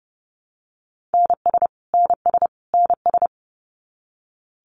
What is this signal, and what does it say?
Morse code, heard as a keyed tone, about 30 wpm, 700 Hz. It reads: NS NS NS
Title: Canadian Provinces and Territories - Repeated 3x - 30wpm